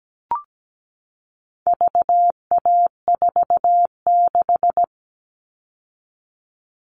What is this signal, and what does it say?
Morse code, heard as a keyed tone, about 17 wpm, 700 Hz. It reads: VA46
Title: All Cut Numbers - 4-Digit 17wpm